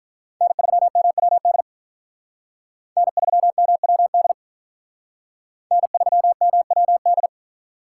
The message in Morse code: N3MWD N3MWD N3MWD

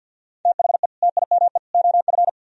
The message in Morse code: THE TIME OF

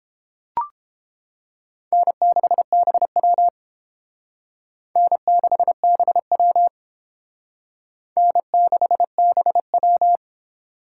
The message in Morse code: N6BW N6BW N6BW

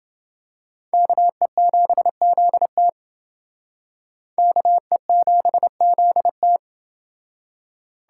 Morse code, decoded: KE7ZT KE7ZT

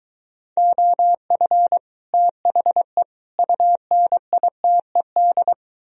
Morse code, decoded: OF THE UNITED

